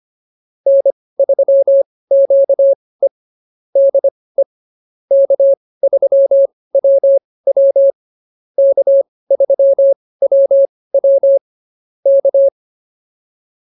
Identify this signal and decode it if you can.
Morse code, heard as a keyed tone, about 24 wpm, spaced as for 18 wpm, 550 Hz. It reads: N3QE DE K3WW K3WW K